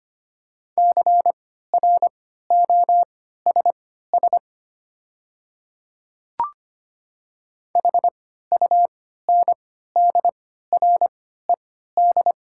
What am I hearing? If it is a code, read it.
Morse code, heard as a keyed tone, about 25 wpm, spaced as for 14 wpm, 700 Hz. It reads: CROSS HUNDRED
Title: 3x Character Spacing - Top 500 Words - 25wpm